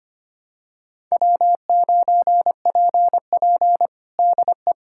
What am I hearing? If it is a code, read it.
Morse code, heard as a keyed tone, about 25 wpm, 700 Hz. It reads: W9PP DE
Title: USA Callsign DE Callsign - 4 Characters - 25wpm